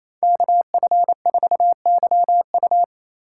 Morse code, heard as a keyed tone, about 33 wpm, 700 Hz. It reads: KF4YU
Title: Speed Racing - USA 2x2 Call Signs 28wpm